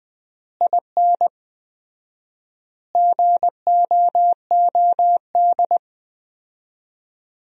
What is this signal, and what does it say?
Morse code, heard as a keyed tone, about 20 wpm, 700 Hz. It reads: IN GOOD
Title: 4x Word-Spacing - Sentences from Top 200 Words - 20wpm